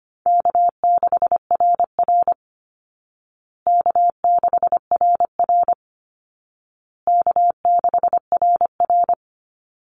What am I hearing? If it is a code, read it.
Morse code, heard as a keyed tone, about 25 wpm, 700 Hz. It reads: K6RR K6RR K6RR
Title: Call Signs Repeated 2x - 25wpm